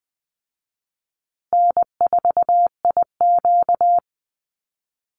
Morse code, heard as a keyed tone, about 19 wpm, 700 Hz. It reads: N4IQ
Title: Easy Call Signs Repeated 3x - 20wpm